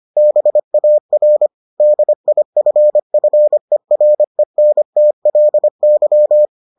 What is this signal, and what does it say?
Morse code, heard as a keyed tone, about 25 wpm, 600 Hz. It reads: BAR DIFFERENTLY